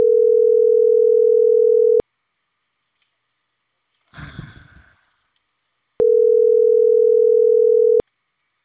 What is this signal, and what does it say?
Ringback tone (North America), 440 + 480 Hz.